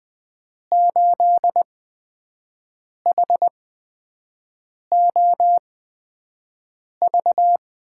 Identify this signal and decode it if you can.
Morse code, heard as a keyed tone, about 20 wpm, spaced as for 5 wpm, 700 Hz. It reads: 8HOV
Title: USA Call Signs - 5 Characters - 8x Character Spacing - 20wpm